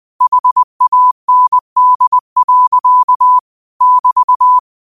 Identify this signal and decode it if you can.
Morse code, heard as a keyed tone, about 20 wpm, 1000 Hz. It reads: HAND. <BT>